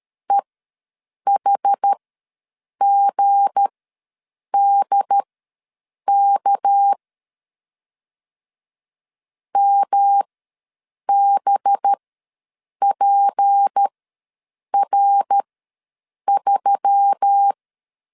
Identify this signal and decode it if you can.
Morse code, heard as a keyed tone, about 13 wpm, spaced as for 7 wpm, 800 Hz. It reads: EHGDK MBPR3
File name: LEZIONI / 7 GF9Z / 7 GF ws 12wpm 3140